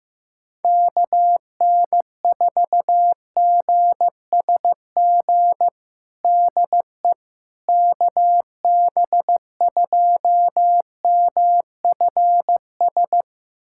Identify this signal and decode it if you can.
Morse code, heard as a keyed tone, about 16 wpm, 700 Hz. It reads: KN4GSG DE KB2MFS
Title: USA Callsign DE Callsign - 6 Characters - 15wpm